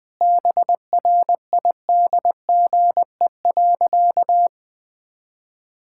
Morse code, heard as a keyed tone, about 20 wpm, 700 Hz. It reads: BRIDGE.
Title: Sentences with 6-Word Phrases - 20wpm